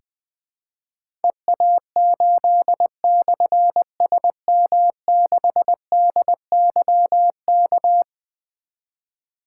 Morse code, heard as a keyed tone, about 20 wpm, 700 Hz. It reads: EA8/SM6DYK
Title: Difficult Call Signs 20wpm